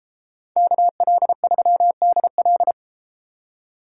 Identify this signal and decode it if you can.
Morse code, heard as a keyed tone, about 32 wpm, 700 Hz. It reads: KL3DL